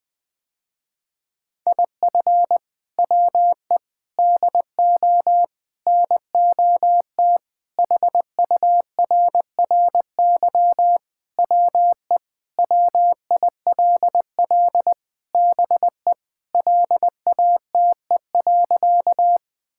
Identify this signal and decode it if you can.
Morse code, heard as a keyed tone, about 20 wpm, 700 Hz. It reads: IF WE DO NOT HURRY WE WILL BE LATE.
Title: Sentences with 4-Word Phrases - Spoken Beginning - 20wpm